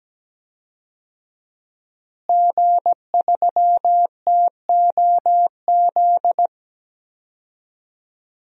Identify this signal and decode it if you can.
Morse code, heard as a keyed tone, about 17 wpm, 700 Hz. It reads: G3TOZ
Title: DXCC-100 Call Signs - 5 Characters - Repeated 3x - 17wpm